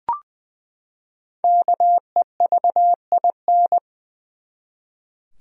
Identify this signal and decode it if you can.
Morse code, heard as a keyed tone, about 20 wpm, 700 Hz. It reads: KEVIN
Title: USA Common Names - Max 5 Letters - 20wpm